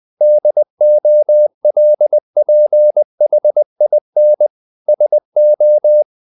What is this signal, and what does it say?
Morse code, heard as a keyed tone, about 20 wpm, 600 Hz. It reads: DOLPHIN SO
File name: the old man and the sea part 2 20wpm